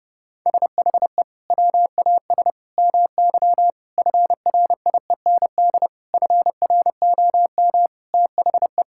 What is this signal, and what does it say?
Morse code, heard as a keyed tone, about 30 wpm, 700 Hz. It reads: SHE WAS MY FRIEND FROM THE
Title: Sentences from Top 1000 Words - Review Part 5 - 30wpm